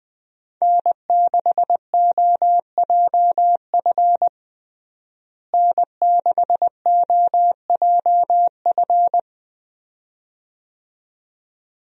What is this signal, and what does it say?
Morse code, heard as a keyed tone, about 20 wpm, 700 Hz. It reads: N6OJF N6OJF
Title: USA Call Signs - 5 Characters - Repeated 2x - 20wpm